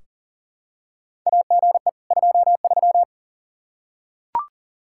Morse code, heard as a keyed tone, about 40 wpm, 700 Hz. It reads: AGE 23
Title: K1USN SST QSO Practice 40wpm